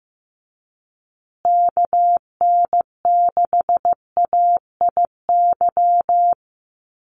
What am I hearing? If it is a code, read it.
Morse code, heard as a keyed tone, about 15 wpm, 700 Hz. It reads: KN6AIY
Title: USA Call Signs Repeated 2x 15wpm